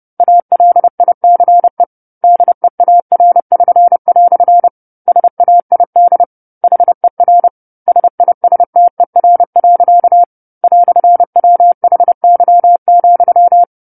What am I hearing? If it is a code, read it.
Morse code, heard as a keyed tone, about 30 wpm, 700 Hz. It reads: ALICE DEAR<SN>" SAID HER SISTER. "WHY,
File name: 10-Chapter-10 30 WPM